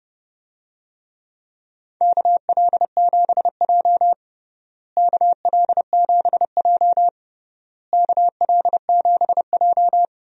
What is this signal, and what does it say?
Morse code, heard as a keyed tone, about 30 wpm, 700 Hz. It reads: KL7J KL7J KL7J
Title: DXCC-100 Call Signs - 4 Characters - Repeated 3x - 30wpm